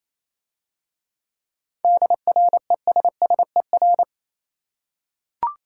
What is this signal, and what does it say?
Morse code, heard as a keyed tone, about 27 wpm, 690 Hz. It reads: DRESSER
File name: Easy Words - Speed-Racing with Hints - 22wpm